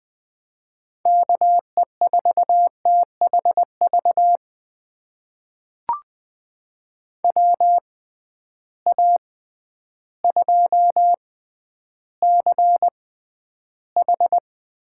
Morse code, heard as a keyed tone, about 20 wpm, spaced as for 11 wpm, 700 Hz. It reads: KE4THV WA2CH
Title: USA Call Signs - 6 Characters - 6x Character Spacing - 20wpm